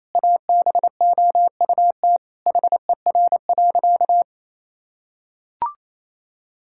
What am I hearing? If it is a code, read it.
Morse code, heard as a keyed tone, about 28 wpm, 700 Hz. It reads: ABOUT HER.
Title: Sentences from Top 1000 Words - Review Part 5 - 28wpm